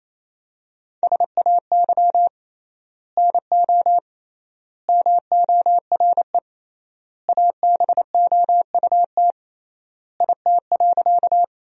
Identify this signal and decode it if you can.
Morse code, heard as a keyed tone, about 28 wpm, 700 Hz. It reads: SAY NO MORE ABOUT IT.
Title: 3x Word-Spacing - Sentences from Top 200 Words - 28wpm